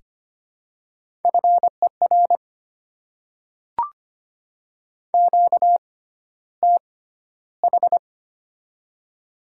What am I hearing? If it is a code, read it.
Morse code, heard as a keyed tone, about 25 wpm, spaced as for 13 wpm, 700 Hz. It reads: FER QTH
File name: Top 20 QSO Elements - 6x Character Spacing - 25wpm